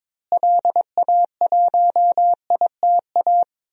Morse code, heard as a keyed tone, about 22 wpm, 700 Hz. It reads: LA1ITA
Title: DXCC-100 Call Signs - 6 Characters - Repeated 3x - 22wpm